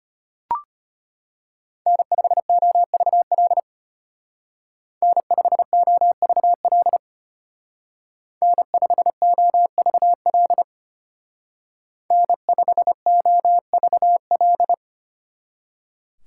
Morse code, heard as a keyed tone, about 30 wpm, 700 Hz. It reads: N5OVL N5OVL N5OVL N5OVL
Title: Speed Racing - USA Call Signs 25wpm